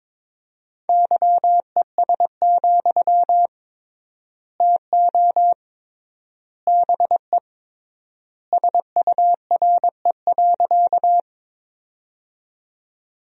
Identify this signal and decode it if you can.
Morse code, heard as a keyed tone, about 22 wpm, 700 Hz. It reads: YES, TO BE SURE.
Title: 3x Word-Spacing - Sentences from Top 500 Words - 22wpm